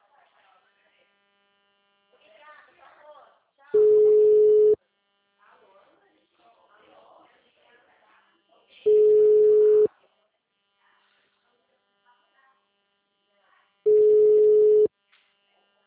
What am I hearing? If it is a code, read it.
Ringback tone (Europe, 425 Hz), with speech.